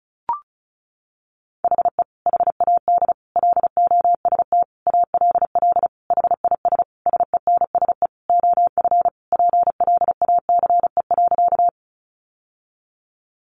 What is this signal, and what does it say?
Morse code, heard as a keyed tone, about 34 wpm, 700 Hz. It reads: HE HAD LOST ALL HIS SENSE OF PLACE.
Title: Sentences from Top 1400 Words 35wpm